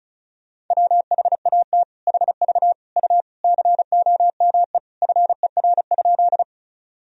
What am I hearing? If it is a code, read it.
Morse code, heard as a keyed tone, about 36 wpm, 700 Hz. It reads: WHAT HV U COME FER?
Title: Sentences from Top 100 Words and Abbreviations 35wpm